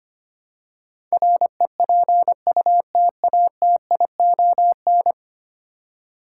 Morse code, heard as a keyed tone, about 25 wpm, 700 Hz. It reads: REPUTATION
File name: 10 Letter Words - First 3 Letters Given - 25wpm